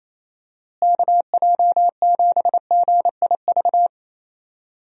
Morse code, heard as a keyed tone, about 28 wpm, 700 Hz. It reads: KJ7GIV
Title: USA Call Signs - 6 Characters - Repeated 2x - 28wpm